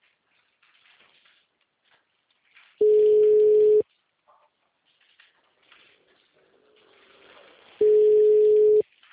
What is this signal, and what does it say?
Ringback tone (Europe, 425 Hz).
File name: out- -2017- - - .156